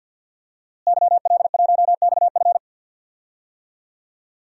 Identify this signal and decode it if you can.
Morse code, heard as a keyed tone, about 53 wpm, 700 Hz. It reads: YL1XF
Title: DXCC-100 Call Signs - 5 Characters - Repeated 3x - 50wpm